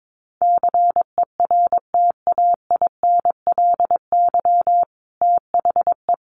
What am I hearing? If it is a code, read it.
Morse code, heard as a keyed tone, about 22 wpm, 700 Hz. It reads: CERTAINLY THE
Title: Sentences from Top 1200 Words 22wpm